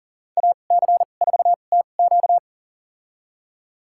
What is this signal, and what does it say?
Morse code, heard as a keyed tone, about 40 wpm, spaced as for 29 wpm, 700 Hz. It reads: AC4TQ
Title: USA Call Signs - 5 Characters - 2x Character Spacing - 40wpm